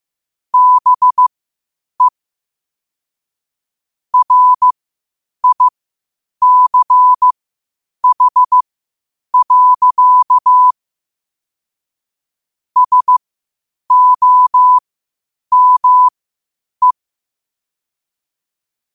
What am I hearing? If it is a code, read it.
Morse code, heard as a keyed tone, about 15 wpm, spaced as for 8 wpm, 1000 Hz. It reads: BE RICH. SOME
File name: qotdfiles / 8wpm